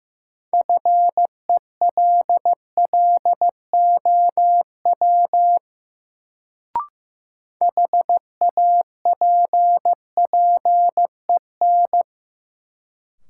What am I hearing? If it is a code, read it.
Morse code, heard as a keyed tone, about 15 wpm, 700 Hz. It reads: FELLOW HAPPEN
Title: Sets of 5 Words Encore - New Word Drill - 15wpm